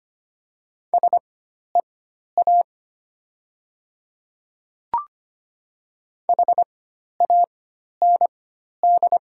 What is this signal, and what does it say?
Morse code, heard as a keyed tone, about 25 wpm, spaced as for 12 wpm, 700 Hz. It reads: SEA HAND
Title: 4x Character Spacing - Top 300 Words - 25wpm